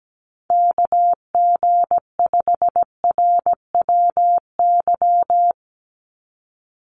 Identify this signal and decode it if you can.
Morse code, heard as a keyed tone, about 17 wpm, 700 Hz. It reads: KG5RWY
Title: USA Call Signs 17wpm